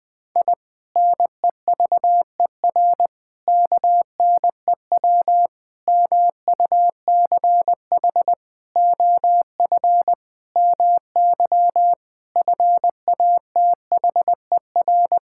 Morse code, heard as a keyed tone, about 21 wpm, 700 Hz. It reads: I NEVER KNEW MUCH OF MY FATHER
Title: Sentences from Top 1000 Words - Review Part 6 - 20wpm